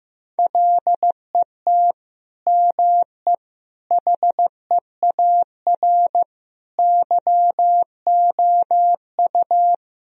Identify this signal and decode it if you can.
Morse code, heard as a keyed tone, about 15 wpm, 700 Hz. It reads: LET ME HEAR YOU
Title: Sentences from Top 1000 Words - Review Part 2 - 15wpm